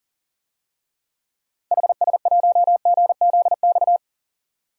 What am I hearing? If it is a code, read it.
Morse code, heard as a keyed tone, about 40 wpm, 700 Hz. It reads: HS1GZX
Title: DXCC-100 Call Signs - 6 Characters - 40wpm